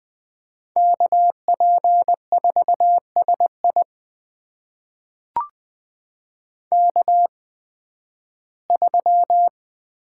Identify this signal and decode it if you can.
Morse code, heard as a keyed tone, about 20 wpm, spaced as for 13 wpm, 700 Hz. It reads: KP4SI K3
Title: USA Call Signs - 5 Characters - 8x Character Spacing - 20wpm